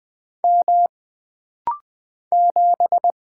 Morse code, heard as a keyed tone, about 20 wpm, 700 Hz. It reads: M 7
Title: Single Character (Letter:Number:Punctuation:Prosign) - Rapid-Fire - 20wpm